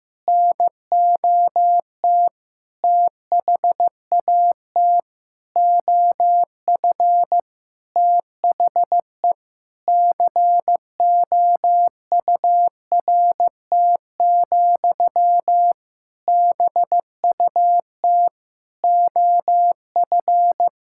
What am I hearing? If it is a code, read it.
Morse code, heard as a keyed tone, about 16 wpm, 700 Hz. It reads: NOT THAT OF THE COURT, BUT OF
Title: Sentences from Top 1600 Words 15wpm